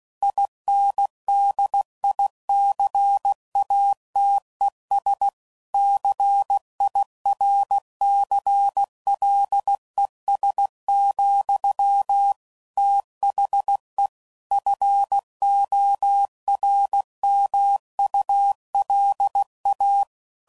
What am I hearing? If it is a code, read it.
Morse code, heard as a keyed tone, about 16 wpm, 800 Hz. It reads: INDICATES CIRCLES, THE FORMULA
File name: CW Code Groups @15WPM / Technical Subjects / XL